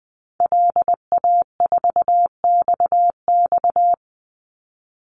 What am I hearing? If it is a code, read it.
Morse code, heard as a keyed tone, about 20 wpm, 700 Hz. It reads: LA4XX
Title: Call Signs Repeated 2x - 20wpm